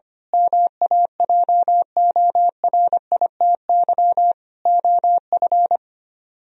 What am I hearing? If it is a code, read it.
Morse code, heard as a keyed tone, about 25 wpm, 700 Hz. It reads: MAJORITY OF
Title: Speed-Racing - Sets of 4 Words 25wpm